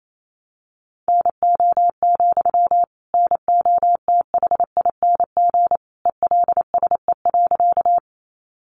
Morse code, heard as a keyed tone, about 28 wpm, 700 Hz. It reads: NO, NOTHING ELSE.
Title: Sentences from Top 800 Words 28wpm